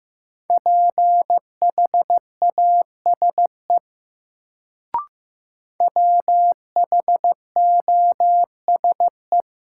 Morse code, heard as a keyed tone, about 15 wpm, 700 Hz. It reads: PHASE WHOSE
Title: Sentences from Top 1200 Words - New Word Drill - 15wpm